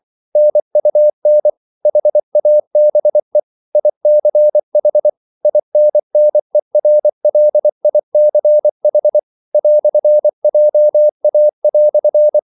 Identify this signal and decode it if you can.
Morse code, heard as a keyed tone, about 24 wpm, 600 Hz. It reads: NUN HABE ICH INNERLICH "JA"